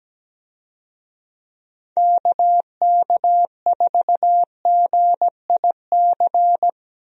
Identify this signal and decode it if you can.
Morse code, heard as a keyed tone, about 17 wpm, 700 Hz. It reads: KK4GIC
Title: USA Call Signs - 6 Characters - Repeated 2x - 17wpm